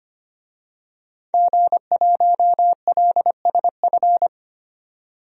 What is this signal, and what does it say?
Morse code, heard as a keyed tone, about 25 wpm, 700 Hz. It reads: G1LSF
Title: DXCC-100 Call Signs - 5 Characters - 25wpm